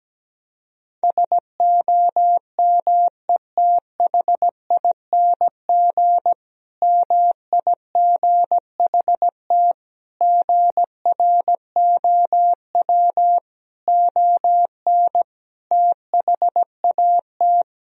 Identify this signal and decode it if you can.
Morse code, heard as a keyed tone, about 17 wpm, 700 Hz. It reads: SOMETHING MIGHT GROW ON THAT
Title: Sentences from Top 1000 Words - Review Part 1 - 17wpm